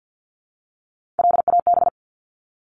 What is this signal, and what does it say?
Morse code, heard as a keyed tone, about 48 wpm, 700 Hz. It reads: LAB
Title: Top 1751-2000 Words 50wpm